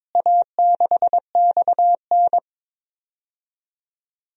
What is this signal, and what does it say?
Morse code, heard as a keyed tone, about 22 wpm, 700 Hz. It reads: A6XN
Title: DXCC-100 Call Signs - 4 Characters - Repeated 3x - 22wpm